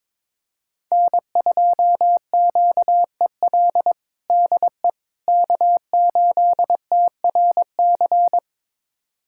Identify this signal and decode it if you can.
Morse code, heard as a keyed tone, about 22 wpm, 700 Hz. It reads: N2QEL DE K8TRC
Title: USA Callsign DE Callsign - 5 Characters - 22wpm